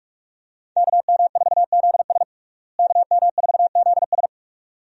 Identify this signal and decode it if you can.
Morse code, heard as a keyed tone, about 48 wpm, 700 Hz. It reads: KM4ZS KM4ZS
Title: USA Call Signs - 5 Characters - Repeated 2x - 45wpm